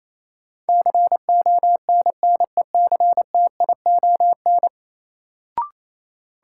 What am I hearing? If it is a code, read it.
Morse code, heard as a keyed tone, about 27 wpm, 700 Hz. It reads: CONNECTION
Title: Sentences from Top 1300 Words - New Word Drill - 28wpm